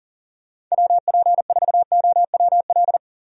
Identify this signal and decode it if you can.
Morse code, heard as a keyed tone, about 40 wpm, 700 Hz. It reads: WP4OWL
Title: USA Call Signs - 6 Characters - Repeated 2x - 40wpm